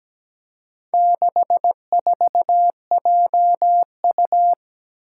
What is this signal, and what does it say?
Morse code, heard as a keyed tone, about 17 wpm, 700 Hz. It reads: 64JU
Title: Four Challenging Letter-Number Combinations 17wpm